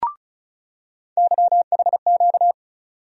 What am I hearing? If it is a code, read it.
Morse code, heard as a keyed tone, about 36 wpm, 700 Hz. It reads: YHQ
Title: USA Call Sign Postfixes - 3 Letters 35wpm